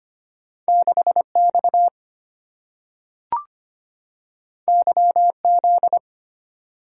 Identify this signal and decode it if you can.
Morse code, heard as a keyed tone, about 25 wpm, 690 Hz. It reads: 6X YZ
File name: Two Letter-Number Combinations - 25wpm